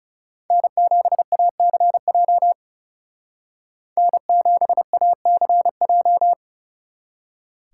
Morse code, heard as a keyed tone, about 34 wpm, 700 Hz. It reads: N7ACJ N7ACJ
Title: Speed Racing - USA Call Signs 30wpm